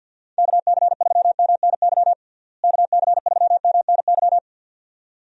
Morse code, heard as a keyed tone, about 52 wpm, 700 Hz. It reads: KC3MNY KC3MNY
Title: USA Call Signs - 6 Characters - Repeated 2x - 50wpm